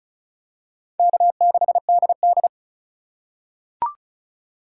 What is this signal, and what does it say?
Morse code, heard as a keyed tone, about 34 wpm, 700 Hz. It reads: K6DD
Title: USA Call Signs - 4 Characters - 2x Character Spacing - 35wpm